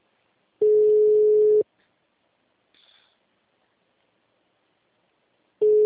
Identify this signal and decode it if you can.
Ringback tone (Europe, 425 Hz).
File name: out- -2003- - - .88198